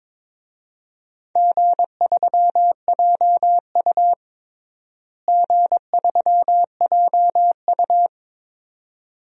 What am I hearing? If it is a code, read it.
Morse code, heard as a keyed tone, about 22 wpm, 700 Hz. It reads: G3JU G3JU
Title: DXCC-100 Call Signs - 4 Characters - Repeated 2x - 22wpm